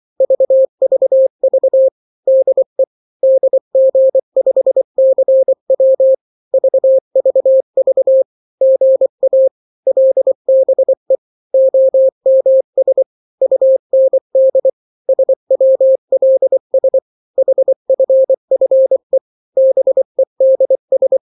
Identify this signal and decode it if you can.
Morse code, heard as a keyed tone, about 24 wpm, 530 Hz. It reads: VVV DE DG5CW VVV GA LBE OMS UND SWLS HFFE BEDS
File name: daily / 23wpm